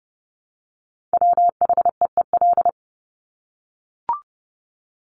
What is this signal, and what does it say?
Morse code, heard as a keyed tone, about 30 wpm, 700 Hz. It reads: WHEEL
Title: Top 500 Words Repeated 30wpm